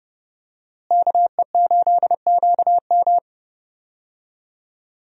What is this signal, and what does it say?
Morse code, heard as a keyed tone, about 31 wpm, 700 Hz. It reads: KE8QM
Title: USA Call Signs - 5 Characters - Repeated 3x - 30wpm